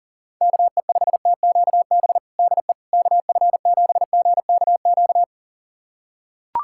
Morse code, heard as a keyed tone, about 40 wpm, 700 Hz. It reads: KE5TQB DE KF7GKQ
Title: USA Callsign DE Callsign - 6 Characters - 40wpm